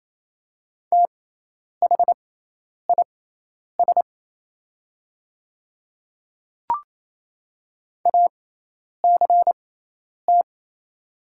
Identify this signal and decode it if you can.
Morse code, heard as a keyed tone, about 28 wpm, spaced as for 10 wpm, 690 Hz. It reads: THIS ACT
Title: 6x Character Spacing - Top 200 Words - 28wpm